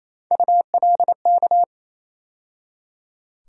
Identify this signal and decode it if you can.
Morse code, heard as a keyed tone, about 27 wpm, 700 Hz. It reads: ULK
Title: USA Call Sign Postfixes - 3 Letters 28wpm